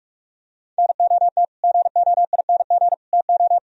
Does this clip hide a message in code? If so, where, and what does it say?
Yes: Morse code from 0.8 to 3.6 s, heard as a keyed tone, about 44 wpm, 700 Hz. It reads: NOT GOING TO